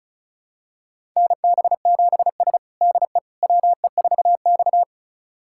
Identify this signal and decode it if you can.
Morse code, heard as a keyed tone, about 34 wpm, 700 Hz. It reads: NB7S DE WE4X
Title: USA Callsign DE Callsign - 4 Characters - 35wpm